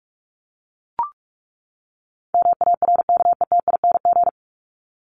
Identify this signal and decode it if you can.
Morse code, heard as a keyed tone, about 44 wpm, 700 Hz. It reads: MARKETING
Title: ING Suffix - Top 100 Words with Base Words 45wpm